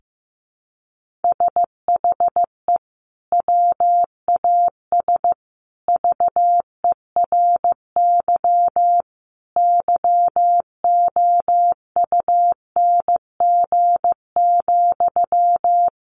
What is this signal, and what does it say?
Morse code, heard as a keyed tone, about 15 wpm, 700 Hz. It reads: SHE WAS VERY YOUNG,
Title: Sentences from Top 1400 Words 15wpm